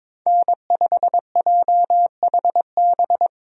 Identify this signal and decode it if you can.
Morse code, heard as a keyed tone, about 22 wpm, 700 Hz. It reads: N5JHB
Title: USA Call Signs - 5 Characters - Repeated 3x - 22wpm